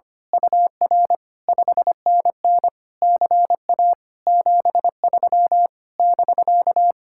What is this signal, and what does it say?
Morse code, heard as a keyed tone, about 24 wpm, 700 Hz. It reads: UR 5NN CA 73 <BK>
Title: Speed-Racing Common QSO Phrases - 25wpm